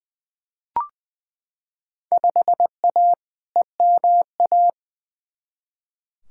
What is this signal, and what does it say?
Morse code, heard as a keyed tone, about 20 wpm, 700 Hz. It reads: 5A EMA
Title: Field Day Exchanges Encore - 20wpm